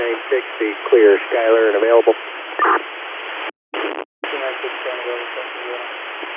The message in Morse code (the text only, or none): none